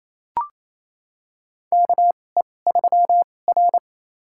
Morse code, heard as a keyed tone, about 28 wpm, spaced as for 20 wpm, 700 Hz. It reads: KE3R
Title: USA Call Signs - 4 Characters - 2x Character Spacing - 28wpm